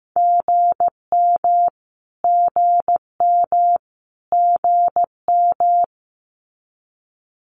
Morse code, heard as a keyed tone, about 15 wpm, 700 Hz. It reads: GM GM GM
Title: Abbreviations Repeated 3x - 15wpm